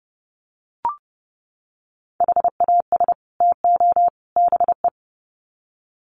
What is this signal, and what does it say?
Morse code, heard as a keyed tone, about 30 wpm, 700 Hz. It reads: HAS TO BE
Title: Sets of 3 Words 30wpm